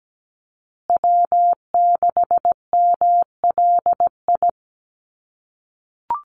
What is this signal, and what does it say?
Morse code, heard as a keyed tone, about 17 wpm, 700 Hz. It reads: W6MLI
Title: USA Call Signs Repeated 2x 17wpm